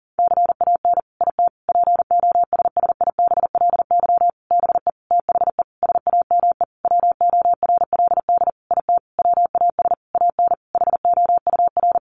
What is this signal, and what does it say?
Morse code, heard as a keyed tone, about 40 wpm, 700 Hz. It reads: CAN IT POSSIBLY BE THE SAME WORLD IT WAS AN HOUR
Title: Sentences from Top 1300 Words 40wpm